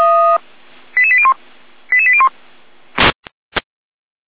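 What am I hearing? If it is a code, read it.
Five-tone selective call (ZVEI), heard as quick runs of five tones, tones of about 70 ms. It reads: 88801 88801
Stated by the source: ZVEI five-tone selective call received by SDR.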